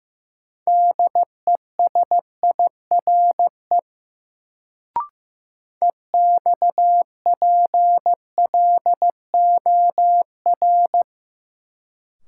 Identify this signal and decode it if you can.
Morse code, heard as a keyed tone, about 15 wpm, 700 Hz. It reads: DESIRE EXPLOR
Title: Sentences from Top 1100 Words - New Word Drill Alt - 15wpm